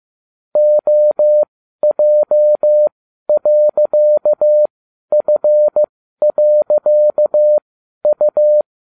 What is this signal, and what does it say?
Morse code, heard as a keyed tone, about 15 wpm, spaced as for 12 wpm, 600 Hz. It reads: OJ.F.U